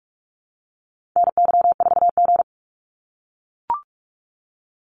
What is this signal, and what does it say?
Morse code, heard as a keyed tone, about 44 wpm, 700 Hz. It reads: NY4G
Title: NA SOTA Activator Callsigns 45wpm